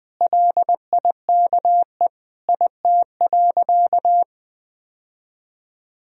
Morse code, heard as a keyed tone, about 21 wpm, 700 Hz. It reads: LIKE IT.
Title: Sentences from Top 1000 Words - Review Part 1 - 20wpm